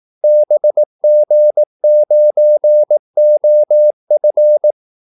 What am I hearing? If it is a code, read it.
Morse code, heard as a keyed tone, about 18 wpm, 600 Hz. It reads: BG9OF
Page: Kaldesignaler 18 wpm